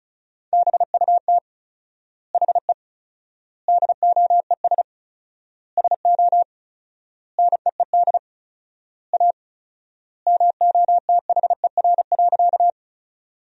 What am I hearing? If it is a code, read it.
Morse code, heard as a keyed tone, about 34 wpm, 700 Hz. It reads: BUT HE DOES SO NEED A MOTHER.
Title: 4x Word-Spacing - Sentences from Top 200 Words - 35wpm